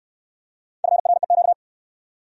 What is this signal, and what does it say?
Morse code, heard as a keyed tone, about 57 wpm, 700 Hz. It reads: FREQ